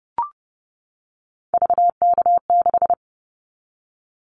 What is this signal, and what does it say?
Morse code, heard as a keyed tone, about 30 wpm, 700 Hz. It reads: VK6
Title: Callsign Prefixes for Top 100 DXCC - 30wpm